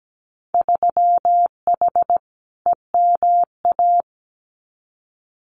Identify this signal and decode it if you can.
Morse code, heard as a keyed tone, about 17 wpm, 700 Hz. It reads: 3H EMA
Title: Winter Field Day Exchanges Encore 17wpm